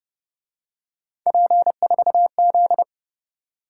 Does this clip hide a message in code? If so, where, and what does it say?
Yes: Morse code from 1.3 to 2.8 s, heard as a keyed tone, about 29 wpm, 700 Hz. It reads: P4Z